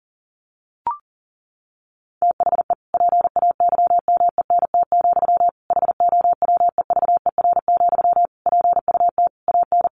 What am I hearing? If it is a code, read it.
Morse code, heard as a keyed tone, about 40 wpm, 700 Hz. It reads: THE PAYMENT, HOWEVER, PUT AN
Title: Sentences from Top 1200 Words 40wpm